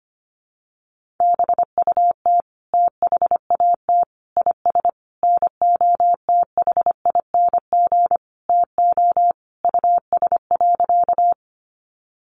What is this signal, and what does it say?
Morse code, heard as a keyed tone, about 25 wpm, 700 Hz. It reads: BUT THAT IS NOTHING TO US.
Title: Sentences from Top 500 Words 25wpm